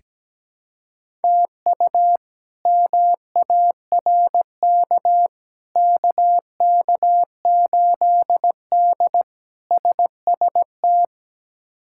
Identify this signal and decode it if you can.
Morse code, heard as a keyed tone, about 17 wpm, 700 Hz. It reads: TU MARK KK8D SST
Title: K1USN SST QSO Practice - 17wpm